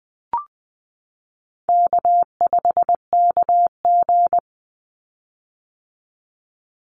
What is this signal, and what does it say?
Morse code, heard as a keyed tone, about 20 wpm, 700 Hz. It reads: K5KG
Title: Call Signs - 20wpm